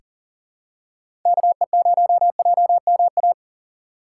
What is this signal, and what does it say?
Morse code, heard as a keyed tone, about 39 wpm, 700 Hz. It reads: KE0JMA
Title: USA Call Signs - 6 Characters - 3x Character Spacing - 40wpm